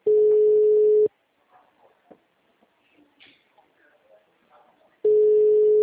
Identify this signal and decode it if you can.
Ringback tone (Europe, 425 Hz), with speech.